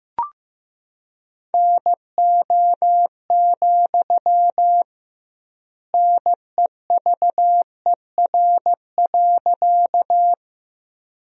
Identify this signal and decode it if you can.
Morse code, heard as a keyed tone, about 16 wpm, 700 Hz. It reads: NO, NEVER.
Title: 2x Word-Spacing - Sentences from Top 300 Words - 15wpm